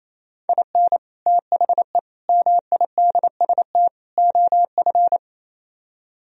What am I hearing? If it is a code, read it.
Morse code, heard as a keyed tone, about 37 wpm, 700 Hz. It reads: IN THE MIDST OF